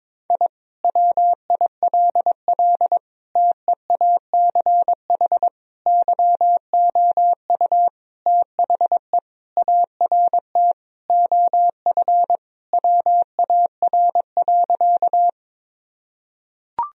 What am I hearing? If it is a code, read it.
Morse code, heard as a keyed tone, about 22 wpm, 700 Hz. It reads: I WILL TEACH YOU THE ART OF WAR.
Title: Sentences from Top 1000 Words - Review Part 3 - 22wpm